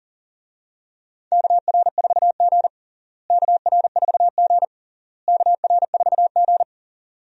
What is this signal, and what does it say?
Morse code, heard as a keyed tone, about 40 wpm, 700 Hz. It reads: KR4G KR4G KR4G
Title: USA Call Signs - 4 Characters - Repeated 3x - 40wpm